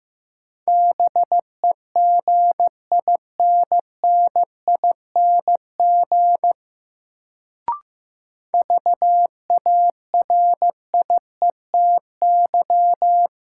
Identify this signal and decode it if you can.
Morse code, heard as a keyed tone, about 15 wpm, 700 Hz. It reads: BEGINNING VARIETY
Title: Sets of 3 Words - New Word Drill - 15wpm